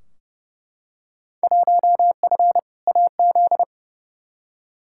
Morse code, heard as a keyed tone, about 30 wpm, 700 Hz. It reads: 1F AZ
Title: Field Day Exchanges - 30wpm